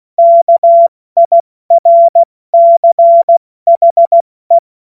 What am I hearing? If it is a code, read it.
Morse code, heard as a keyed tone, about 16 wpm, 680 Hz. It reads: KIRCHE
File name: Morsekurs / Witze / Witz 07.November 2023 70 F80